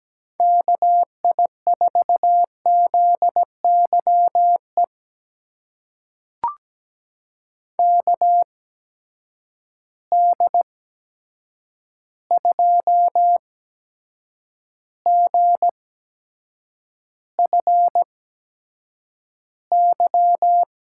Morse code, heard as a keyed tone, about 17 wpm, spaced as for 7 wpm, 700 Hz. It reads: KI4ZYE KD2GFY
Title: USA Call Signs - 8x Character Spacing - 17wpm